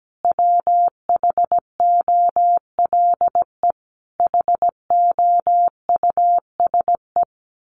Morse code, heard as a keyed tone, about 17 wpm, 700 Hz. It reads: WHOLE HOUSE
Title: Sentences from Top 1200 Words 17wpm